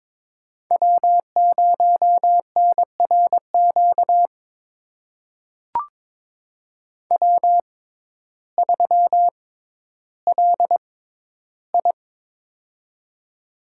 Morse code, heard as a keyed tone, about 22 wpm, spaced as for 12 wpm, 700 Hz. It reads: W0NRQ W3LI
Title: USA Call Signs - 6x Character Spacing - 22wpm